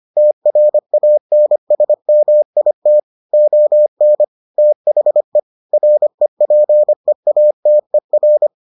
Morse code, heard as a keyed tone, about 25 wpm, 600 Hz. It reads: TRANSMIT ON THE REPEATER